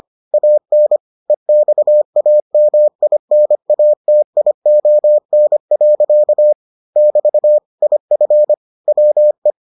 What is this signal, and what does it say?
Morse code, heard as a keyed tone, about 25 wpm, 600 Hz. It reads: AN EXAMINATION. <BT> IF WE